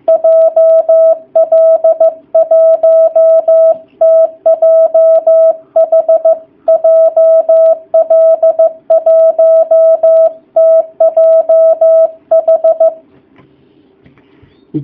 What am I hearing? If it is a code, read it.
Morse code, heard as a keyed tone, about 15 wpm, 660 Hz. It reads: JL1TJH JL1TJH
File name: jh8cbh / 2019cw / sp15- w